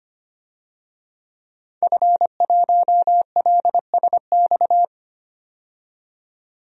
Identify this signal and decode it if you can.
Morse code, heard as a keyed tone, about 25 wpm, 700 Hz. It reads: F1LSX
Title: DXCC-100 Call Signs - 5 Characters - 25wpm